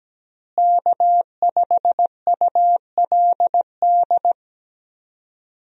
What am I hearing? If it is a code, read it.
Morse code, heard as a keyed tone, about 17 wpm, 700 Hz. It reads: K5ULD
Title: USA Call Signs - 5 Characters - Repeated 2x - 17wpm